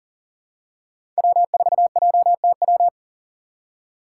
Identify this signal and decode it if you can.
Morse code, heard as a keyed tone, about 40 wpm, 700 Hz. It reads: W4JTW
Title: USA Call Signs - 5 Characters - 2x Character Spacing - 40wpm